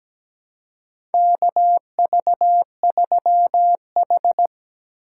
Morse code, heard as a keyed tone, about 17 wpm, 700 Hz. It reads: KV3H
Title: USA Call Signs - 3x Character Spacing - 17wpm